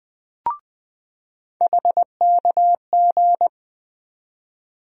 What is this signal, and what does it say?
Morse code, heard as a keyed tone, about 20 wpm, 700 Hz. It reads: HKG
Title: USA Call Sign Postfixes - 3 Letters 20wpm